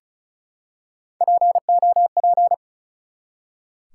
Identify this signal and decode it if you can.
Morse code, heard as a keyed tone, about 36 wpm, 700 Hz. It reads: POP
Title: Three Letter Words - 35wpm